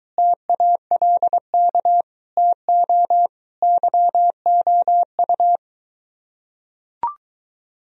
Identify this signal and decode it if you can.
Morse code, heard as a keyed tone, about 23 wpm, 700 Hz. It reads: TALK TO YOU